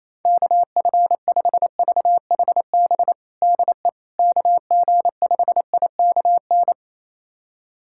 Morse code, heard as a keyed tone, about 28 wpm, 700 Hz. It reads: KF5VHB DE KG5IKN
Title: USA Callsign DE Callsign - 6 Characters - 28wpm